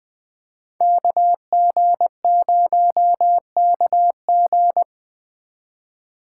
Morse code, heard as a keyed tone, about 20 wpm, 700 Hz. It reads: KG0KG